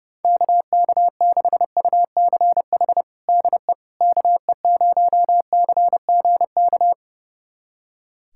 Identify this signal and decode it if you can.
Morse code, heard as a keyed tone, about 30 wpm, 700 Hz. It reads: KK6UCH DE KE0CGK
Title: USA Callsign DE Callsign 30wpm